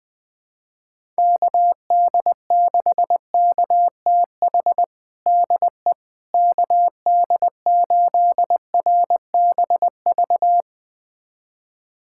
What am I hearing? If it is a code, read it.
Morse code, heard as a keyed tone, about 21 wpm, 700 Hz. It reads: KD6KTH DE KD8RBV
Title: USA Callsign DE Callsign - 6 Characters - 20wpm